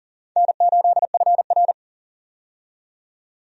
Morse code, heard as a keyed tone, about 40 wpm, 700 Hz. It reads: N8FR
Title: USA Call Signs - 4 Characters - 40wpm